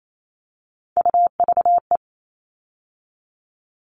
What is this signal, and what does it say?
Morse code, heard as a keyed tone, about 28 wpm, 700 Hz. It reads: UVE
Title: All Cut Numbers - 3-Digit 28wpm